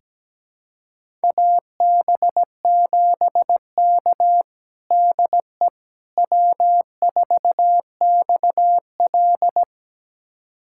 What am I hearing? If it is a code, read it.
Morse code, heard as a keyed tone, about 17 wpm, 700 Hz. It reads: AB7K DE W4XL
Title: USA Callsign DE Callsign - 4 Characters - 17wpm